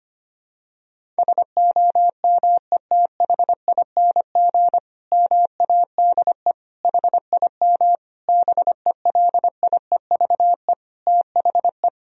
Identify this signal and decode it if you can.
Morse code, heard as a keyed tone, about 25 wpm, 700 Hz. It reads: SOMETHING MADE HIM BELIEVE THE
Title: Sentences from Top 1000 Words - Review Part 1 - 25wpm